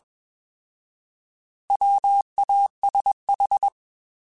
Morse code, heard as a keyed tone, about 21 wpm, 800 Hz. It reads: WASH
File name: Alphabetical / Alphabetical mp3s / Ws @20